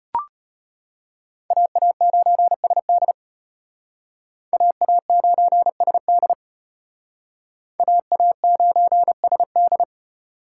Morse code, heard as a keyed tone, about 31 wpm, 700 Hz. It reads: AA9SD AA9SD AA9SD